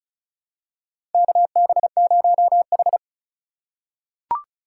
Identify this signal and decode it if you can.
Morse code, heard as a keyed tone, about 36 wpm, 700 Hz. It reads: KB0H
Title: USA Call Signs - 3x Character Spacing - 35wpm